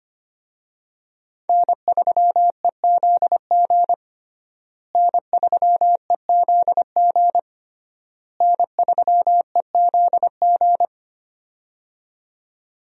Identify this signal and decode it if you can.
Morse code, heard as a keyed tone, about 25 wpm, 700 Hz. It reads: N3EZG N3EZG N3EZG
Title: USA Call Signs - 5 Characters - Repeated 3x - 25wpm